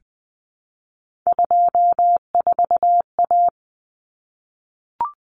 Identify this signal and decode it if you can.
Morse code, heard as a keyed tone, about 20 wpm, 700 Hz. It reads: 24A
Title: Common Cut Numbers - 3-Digit 20wpm